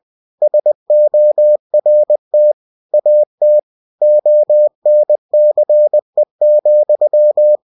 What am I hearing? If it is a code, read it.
Morse code, heard as a keyed tone, about 20 wpm, 600 Hz. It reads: SORT AT ONCE,
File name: cw-books / winnie-the-pooh / winnie-20wpm-page-0048